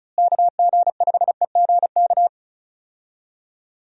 Morse code, heard as a keyed tone, about 36 wpm, 700 Hz. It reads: KG5EGK
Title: USA Call Signs - 6 Characters - Repeated 2x - 35wpm